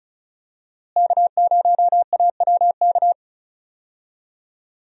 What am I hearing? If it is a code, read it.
Morse code, heard as a keyed tone, about 36 wpm, 700 Hz. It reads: K0AWK
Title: USA Call Signs - 5 Characters - Repeated 3x - 35wpm